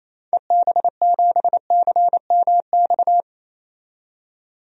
Morse code, heard as a keyed tone, about 28 wpm, 700 Hz. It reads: EB7CMX
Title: DXCC-100 Call Signs - 6 Characters - 28wpm